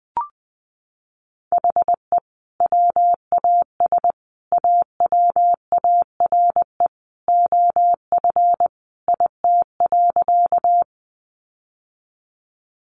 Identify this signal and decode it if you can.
Morse code, heard as a keyed tone, about 20 wpm, 700 Hz. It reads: HE WAS AWARE OF IT.
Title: Sentences from Top 1200 Words 20wpm